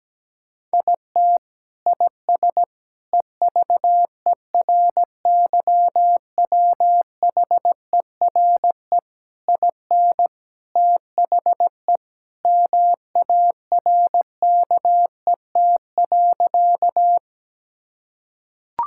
Morse code, heard as a keyed tone, about 17 wpm, 700 Hz. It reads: IT IS EVERYWHERE IN THE MARKET.
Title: Sentences from Top 1600 Words 17wpm